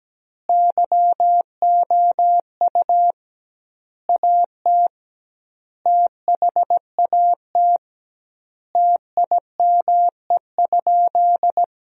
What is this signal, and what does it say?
Morse code, heard as a keyed tone, about 17 wpm, 700 Hz. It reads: YOU AT THAT TIME?
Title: 2x Word-Spacing - Sentences from Top 200 Words - 17wpm